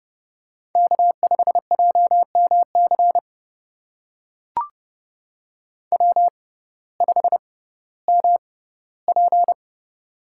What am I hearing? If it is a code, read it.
Morse code, heard as a keyed tone, about 30 wpm, spaced as for 17 wpm, 700 Hz. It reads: K5JMC W5MP
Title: USA Call Signs - 6x Character Spacing - 30wpm